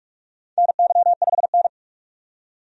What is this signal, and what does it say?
Morse code, heard as a keyed tone, about 42 wpm, 700 Hz. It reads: NY5N
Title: USA Call Signs - 4 Characters - 45wpm